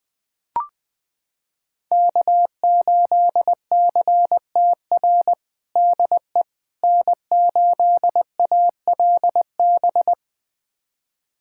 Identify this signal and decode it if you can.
Morse code, heard as a keyed tone, about 20 wpm, 700 Hz. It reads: K8CTR DE N8ALB
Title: USA Callsign DE Callsign - 5 Characters - 20wpm